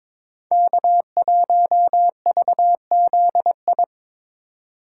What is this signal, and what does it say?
Morse code, heard as a keyed tone, about 26 wpm, 700 Hz. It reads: K1VZI
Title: Speed Racing - USA 1x3 Call Signs 22wpm